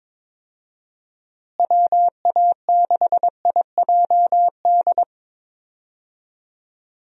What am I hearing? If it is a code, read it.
Morse code, heard as a keyed tone, about 22 wpm, 700 Hz. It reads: WA6IJD
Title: USA Call Signs - 6 Characters - Repeated 3x - 22wpm